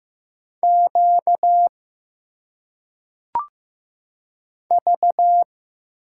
Morse code, heard as a keyed tone, about 15 wpm, 700 Hz. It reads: Q V
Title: Single Letters - Phonetic Pronunciation 15wpm